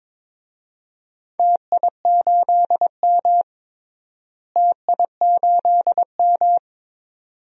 Morse code, heard as a keyed tone, about 22 wpm, 700 Hz. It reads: TI8M TI8M
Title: DXCC-100 Call Signs - 4 Characters - Repeated 2x - 22wpm